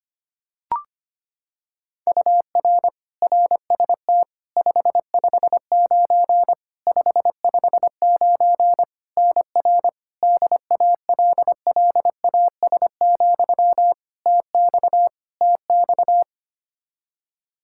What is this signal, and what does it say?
Morse code, heard as a keyed tone, about 25 wpm, 700 Hz. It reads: UR RST 559 559 NR DALLAS, TX TX
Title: Practice QSO - Exchange 1 - Beginning - 25wpm